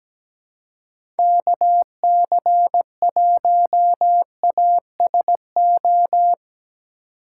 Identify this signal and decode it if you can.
Morse code, heard as a keyed tone, about 17 wpm, 700 Hz. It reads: KC1ASO
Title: USA Call Signs - 6 Characters - 4x Character Spacing - 17wpm